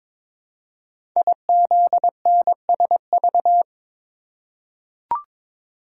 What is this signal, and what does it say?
Morse code, heard as a keyed tone, about 22 wpm, 700 Hz. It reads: IZNSV
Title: Code Groups - 5 Letters - 22wpm